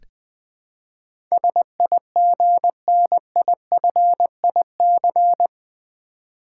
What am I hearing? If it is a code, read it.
Morse code, heard as a keyed tone, about 20 wpm, 700 Hz. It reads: SIGNIFIC
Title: Build A Word - 4 Letters At A Time - 12-Letter Words - 20wpm